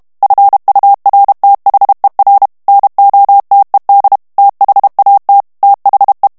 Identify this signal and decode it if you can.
Morse code, heard as a keyed tone, about 32 wpm, 800 Hz. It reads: FURTHER NOTED THAT THE
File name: ANGLAIS texte clair 3 vitesse 30 mots minute